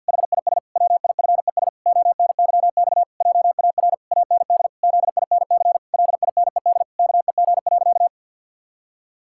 Morse code, heard as a keyed tone, about 49 wpm, 700 Hz. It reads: HIS WIFES ONYX JAR AND ZINC LINED KEG.
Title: Pangram Challenge v2 - 50wpm